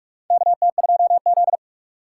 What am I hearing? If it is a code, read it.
Morse code, heard as a keyed tone, about 42 wpm, 700 Hz. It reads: KT2Z
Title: USA Call Signs - 4 Characters - 6x Character Spacing - 45wpm